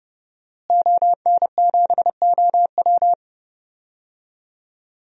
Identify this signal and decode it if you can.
Morse code, heard as a keyed tone, about 31 wpm, 700 Hz. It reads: ON7OW
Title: DXCC-100 Call Signs - 5 Characters - Repeated 3x - 30wpm